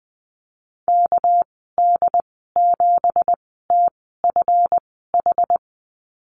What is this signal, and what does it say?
Morse code, heard as a keyed tone, about 20 wpm, spaced as for 14 wpm, 700 Hz. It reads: KD7TFH
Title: 219 Lesson 35g - Number 0 - Call Signs 20wpm